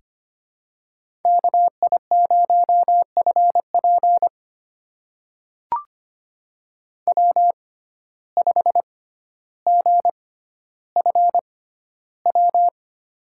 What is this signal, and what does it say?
Morse code, heard as a keyed tone, about 25 wpm, spaced as for 13 wpm, 700 Hz. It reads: KI0FP W5GFW
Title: USA Call Signs - 5 Characters - 6x Character Spacing - 25wpm